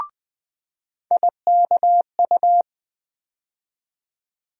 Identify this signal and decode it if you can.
Morse code, heard as a keyed tone, about 20 wpm, 700 Hz. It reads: IKU
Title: USA Call Sign Postfixes - 3 Letters 20wpm